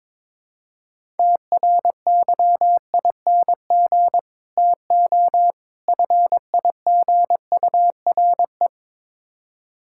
Morse code, heard as a keyed tone, about 22 wpm, 700 Hz. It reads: TRYING TO FIGURE
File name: practice / 22wpm / Build A Phrase - 5-Word Phrases - 22wpm